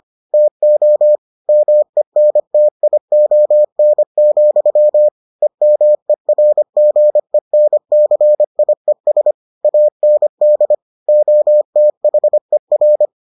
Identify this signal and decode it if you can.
Morse code, heard as a keyed tone, about 25 wpm, 600 Hz. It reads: TO MENTION, EMERGENCIES AND OTHER